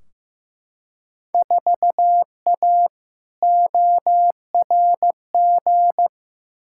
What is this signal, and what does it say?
Morse code, heard as a keyed tone, about 15 wpm, 700 Hz. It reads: 4A ORG
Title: Field Day Exchanges - 15wpm